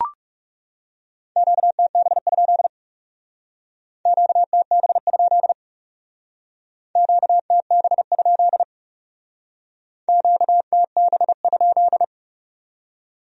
Morse code, heard as a keyed tone, about 39 wpm, 700 Hz. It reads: QTB? QTB? QTB? QTB?